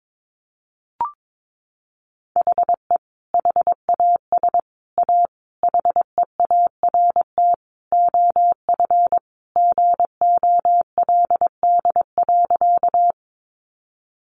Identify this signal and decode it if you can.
Morse code, heard as a keyed tone, about 22 wpm, 700 Hz. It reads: HE HAS A HEART OF GOLD.
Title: Sentences from Top 1200 Words 22wpm